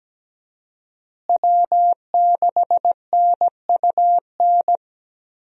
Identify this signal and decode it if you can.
Morse code, heard as a keyed tone, about 17 wpm, 700 Hz. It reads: W6NUN
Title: USA Call Signs - 5 Characters - Repeated 2x - 17wpm